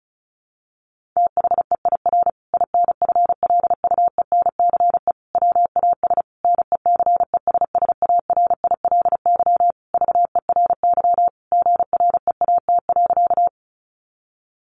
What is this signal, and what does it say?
Morse code, heard as a keyed tone, about 34 wpm, 700 Hz. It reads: THEIR INFLUENCE WAS NECESSARILY VERY GREAT.
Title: Sentences from Top 1400 Words 35wpm